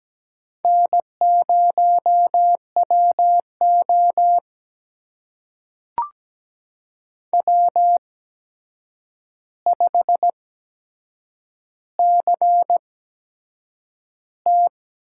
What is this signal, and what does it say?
Morse code, heard as a keyed tone, about 17 wpm, spaced as for 7 wpm, 700 Hz. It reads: N0WO W5CT
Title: USA Call Signs - 4 Characters - 8x Character Spacing - 17wpm